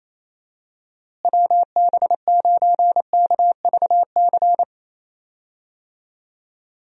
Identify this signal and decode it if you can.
Morse code, heard as a keyed tone, about 28 wpm, 700 Hz. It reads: WB9KVC
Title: USA Call Signs - 6 Characters - Repeated 3x - 28wpm